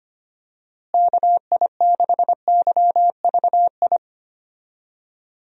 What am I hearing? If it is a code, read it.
Morse code, heard as a keyed tone, about 25 wpm, 700 Hz. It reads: KI6YVI
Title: USA Call Signs - 6 Characters - Repeated 2x - 25wpm